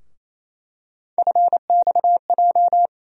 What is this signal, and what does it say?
Morse code, heard as a keyed tone, about 28 wpm, 700 Hz. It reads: FXJ
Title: USA Call Sign Postfixes - 3 Letters 28wpm